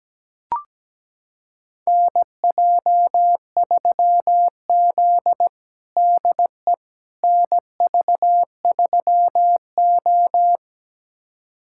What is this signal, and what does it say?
Morse code, heard as a keyed tone, about 17 wpm, 700 Hz. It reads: NJ3Z DE NV3O
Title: USA Callsign DE Callsign - 4 Characters - 17wpm